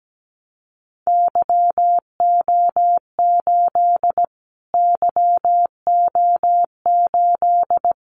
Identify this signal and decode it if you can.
Morse code, heard as a keyed tone, about 17 wpm, 700 Hz. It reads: YO8 YO8
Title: Callsign Prefixes for Top 25 DXCC - Repeated 3x - 17wpm